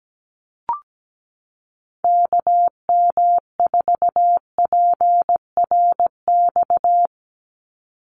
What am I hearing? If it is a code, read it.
Morse code, heard as a keyed tone, about 17 wpm, 700 Hz. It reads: KM4PRX
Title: USA Call Signs 17wpm